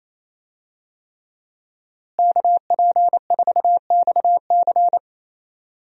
Morse code, heard as a keyed tone, about 27 wpm, 700 Hz. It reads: KP4XC
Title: DXCC-100 Call Signs - 5 Characters - Repeated 3x - 28wpm